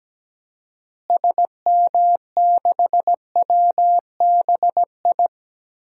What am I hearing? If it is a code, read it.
Morse code, heard as a keyed tone, about 17 wpm, 700 Hz. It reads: SM6WBI
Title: DXCC-100 Call Signs - 6 Characters - Repeated 2x - 17wpm